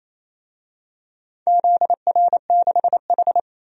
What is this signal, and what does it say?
Morse code, heard as a keyed tone, about 28 wpm, 700 Hz. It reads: ZR6H
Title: DXCC-100 Call Signs - 4 Characters - 28wpm